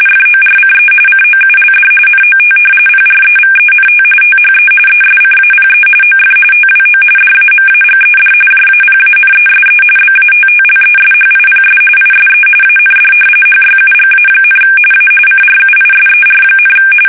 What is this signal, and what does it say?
RTTY, about 45.45 baud. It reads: FORCE COULD NOT PREVENT THE ESCAPE OF TWO DAMAGED JAPANESE BATTLESHIPS ISE AND HYUGA, A CRUISER,
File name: fsk850 / battleshipnj